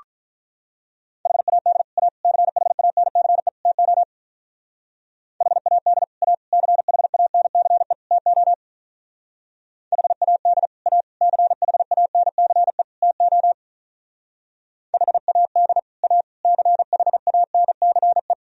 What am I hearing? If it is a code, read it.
Morse code, heard as a keyed tone, about 45 wpm, 700 Hz. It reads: HAD A CHANCE TO HAD A CHANCE TO HAD A CHANCE TO HAD A CHANCE